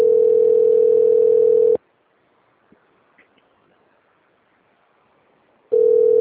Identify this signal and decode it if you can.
Ringback tone (North America), 440 + 480 Hz.